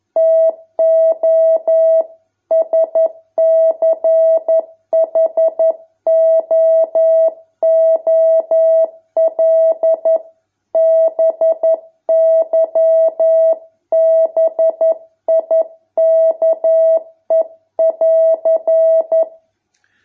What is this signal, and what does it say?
Morse code, heard as a keyed tone, about 11 wpm, 650 Hz. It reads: TO SCHOOL BY BIKE<AR>